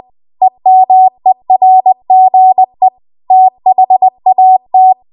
Morse code, heard as a keyed tone, about 20 wpm, 750 Hz. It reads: EMERGE THAT
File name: Archive / 20 WPM / 20WPM